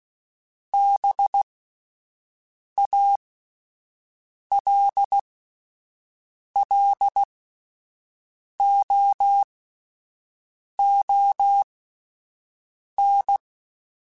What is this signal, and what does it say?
Morse code, heard as a keyed tone, about 16 wpm, spaced as for 5 wpm, 800 Hz. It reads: BALLOON